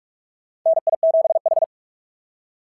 Morse code, heard as a keyed tone, about 44 wpm, 660 Hz. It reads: NI7H
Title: SOTA QSO Practice 45wpm